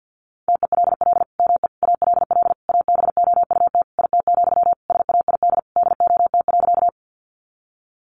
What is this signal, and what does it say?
Morse code, heard as a keyed tone, about 50 wpm, 700 Hz. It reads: TELL ME ALL ABOUT IT, SAID DOT.
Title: Sentences from Top 1500 Words 50wpm